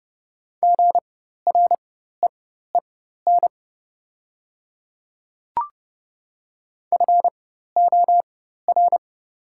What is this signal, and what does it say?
Morse code, heard as a keyed tone, about 31 wpm, spaced as for 14 wpm, 700 Hz. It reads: GREEN FOR
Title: 4x Character Spacing - Top 500 Words - 30wpm